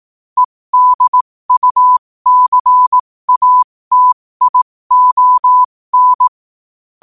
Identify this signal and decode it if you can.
Morse code, heard as a keyed tone, about 18 wpm, spaced as for 15 wpm, 1000 Hz. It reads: EDUCATION